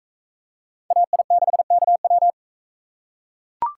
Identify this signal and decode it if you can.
Morse code, heard as a keyed tone, about 43 wpm, 700 Hz. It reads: AI6KW